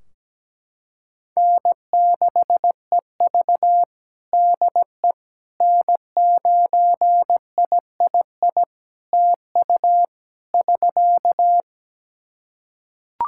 Morse code, heard as a keyed tone, about 17 wpm, 700 Hz. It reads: N6EV DE N9III TU <SK>
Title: Practice QSO - Exchange 4 - Ending - 17wpm